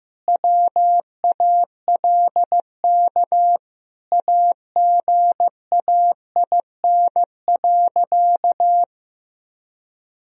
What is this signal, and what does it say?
Morse code, heard as a keyed tone, about 15 wpm, 700 Hz. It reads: WALK AGAIN.
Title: Sentences from Top 1000 Words - Review Part 1 - 15wpm